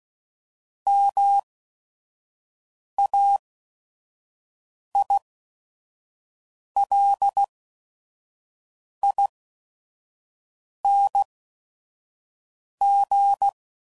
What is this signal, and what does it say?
Morse code, heard as a keyed tone, about 16 wpm, spaced as for 5 wpm, 800 Hz. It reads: MAILING
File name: Suffixes / Special ING's / VerySlowToFaster ING's